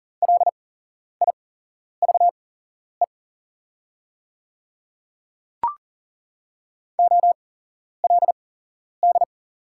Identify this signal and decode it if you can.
Morse code, heard as a keyed tone, about 40 wpm, spaced as for 11 wpm, 690 Hz. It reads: LIVE OLD
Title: 8x Character Spacing - Top 200 Words - 40wpm